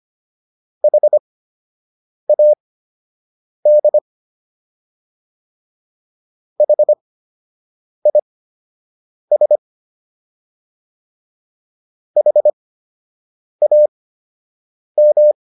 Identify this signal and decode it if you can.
Morse code, heard as a keyed tone, about 25 wpm, spaced as for 7 wpm, 600 Hz. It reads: HAD HIS HAM